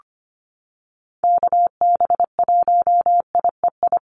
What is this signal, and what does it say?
Morse code, heard as a keyed tone, about 25 wpm, 700 Hz. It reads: KB1IEI
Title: USA Call Signs Repeated 2x 25wpm